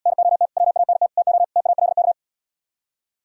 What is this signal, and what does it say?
Morse code, heard as a keyed tone, about 77 wpm, 700 Hz. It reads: NOT WANT TO TALK